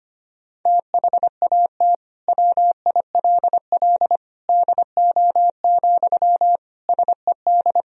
Morse code, heard as a keyed tone, about 25 wpm, 700 Hz. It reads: THAT WILL DO, SED
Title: Sentences from Top 100 Words and Abbreviations 25wpm